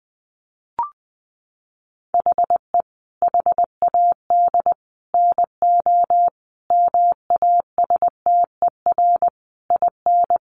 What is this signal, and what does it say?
Morse code, heard as a keyed tone, about 20 wpm, 700 Hz. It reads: HE HAD NO MASTER IN
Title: Sentences from Top 1300 Words 20wpm